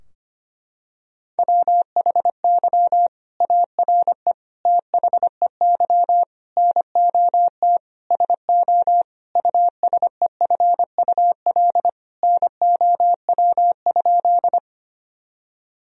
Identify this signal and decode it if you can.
Morse code, heard as a keyed tone, about 25 wpm, 700 Hz. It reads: WHY ARE THEY NOT SO USEFUL NOW?
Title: Sentences from Top 1000 Words - Review Part 4 - 25wpm